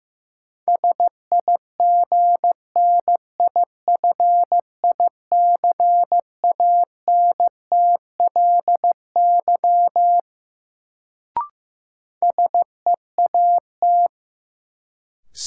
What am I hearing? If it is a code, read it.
Morse code, heard as a keyed tone, about 16 wpm, 700 Hz. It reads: SIGNIFICANTLY SEAT
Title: Sentences from Top 1500 Words - New Word Drill Alt - 15wpm